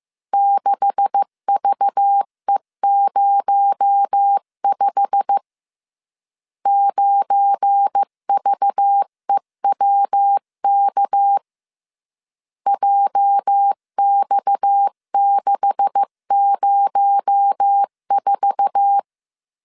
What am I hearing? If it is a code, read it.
Morse code, heard as a keyed tone, about 15 wpm, 800 Hz. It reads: 6VE05 9VEWK JX604
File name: CW / DET / 1 9 1500C S B 14wpm 3302